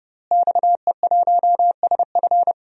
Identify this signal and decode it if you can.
Morse code, heard as a keyed tone, about 32 wpm, 700 Hz. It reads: XE1SF
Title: DXCC-100 Call Signs - 5 Characters - Repeated 3x - 30wpm